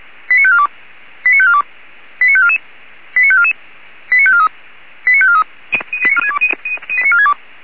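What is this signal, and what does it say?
Five-tone selective call (ZVEI), heard as quick runs of five tones, tones of about 70 ms. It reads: 89542 89542 89544 89544 89543 89543
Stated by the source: ZVEI five-tone selective call received on an SDR, from the Feuerwehr Orlamünde archive.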